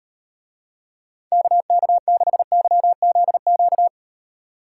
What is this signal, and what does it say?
Morse code, heard as a keyed tone, about 38 wpm, 700 Hz. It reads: KK6YZQ